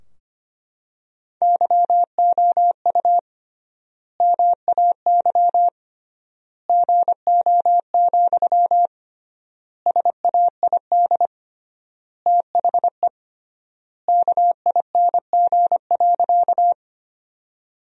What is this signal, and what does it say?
Morse code, heard as a keyed tone, about 25 wpm, 700 Hz. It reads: YOU MAY GO, SAID THE KING.
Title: 3x Word-Spacing - Sentences from Top 400 Words - 25wpm